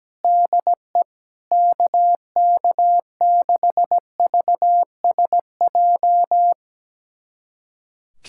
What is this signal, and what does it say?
Morse code, heard as a keyed tone, about 17 wpm, 700 Hz. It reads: DE KK6VSJ
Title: USA Callsign DE Callsign - 6 Characters - 17wpm